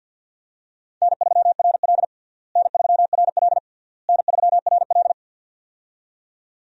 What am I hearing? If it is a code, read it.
Morse code, heard as a keyed tone, about 47 wpm, 700 Hz. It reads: N3RL N3RL N3RL
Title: USA Call Signs - 4 Characters - Repeated 3x - 50wpm